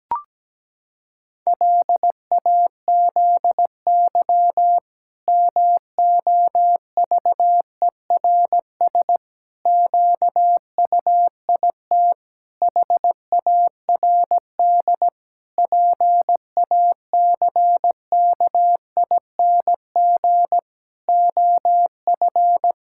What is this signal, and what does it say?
Morse code, heard as a keyed tone, about 17 wpm, 700 Hz. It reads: LAZY MOVERS QUIT HARD PACKING OF
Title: Pangram Challenge v2 - 17wpm